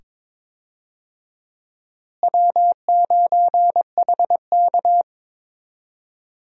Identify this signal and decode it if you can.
Morse code, heard as a keyed tone, about 22 wpm, 700 Hz. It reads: W9HK
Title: USA Call Signs - 4 Characters - Repeated 2x - 22wpm